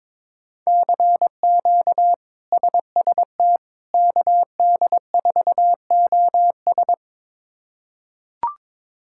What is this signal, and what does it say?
Morse code, heard as a keyed tone, about 21 wpm, 700 Hz. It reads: CQ SST KD4OS
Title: K1USN SST QSO Practice - 22wpm